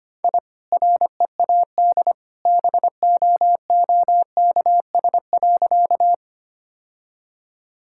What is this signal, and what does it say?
Morse code, heard as a keyed tone, about 25 wpm, 700 Hz. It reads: I READ BOOKS.
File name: Easy Sentences with Hints - 25wpm